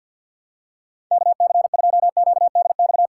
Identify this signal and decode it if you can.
Morse code, heard as a keyed tone, about 53 wpm, 700 Hz. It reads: KC2QDX
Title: USA Call Signs - 6 Characters - 8x Character Spacing - 50wpm